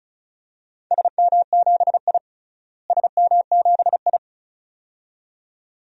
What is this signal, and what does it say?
Morse code, heard as a keyed tone, about 34 wpm, 700 Hz. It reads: SM7I SM7I
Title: DXCC-100 Call Signs - 4 Characters - Repeated 2x - 35wpm